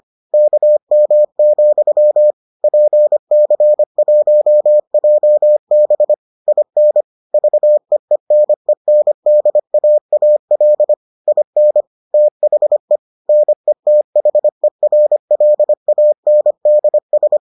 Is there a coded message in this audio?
Yes: Morse code from 0.3 to 17.4 s, heard as a keyed tone, about 25 wpm, 600 Hz. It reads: KM, PC1JB IN VEENENDAAL IN THE NETHERLANDS